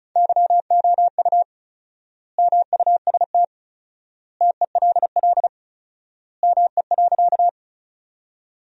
Morse code, heard as a keyed tone, about 36 wpm, 700 Hz. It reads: YOU MUST TELL ME.
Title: 4x Word-Spacing - Sentences from Top 200 Words - 35wpm